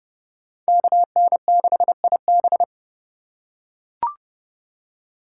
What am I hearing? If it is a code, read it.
Morse code, heard as a keyed tone, about 30 wpm, 700 Hz. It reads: KN6IB
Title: USA Call Signs - 5 Characters - 3x Character Spacing - 30wpm